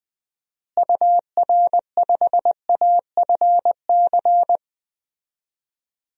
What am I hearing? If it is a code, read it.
Morse code, heard as a keyed tone, about 20 wpm, 700 Hz. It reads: UR5AFC
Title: DXCC-100 Call Signs - 6 Characters - Repeated 2x - 20wpm